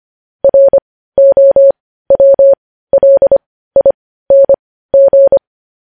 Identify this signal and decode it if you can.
Morse code, heard as a keyed tone, about 25 wpm, spaced as for 15 wpm, 550 Hz. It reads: ROWLING